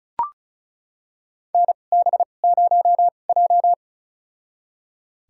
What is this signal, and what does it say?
Morse code, heard as a keyed tone, about 36 wpm, spaced as for 25 wpm, 700 Hz. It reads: NB0J
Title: USA Call Signs - 4 Characters - 2x Character Spacing - 35wpm